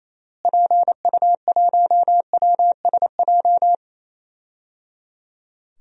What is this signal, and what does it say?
Morse code, heard as a keyed tone, about 28 wpm, 700 Hz. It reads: PU1WSJ
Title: DXCC-100 Call Signs - 6 Characters - Repeated 3x - 28wpm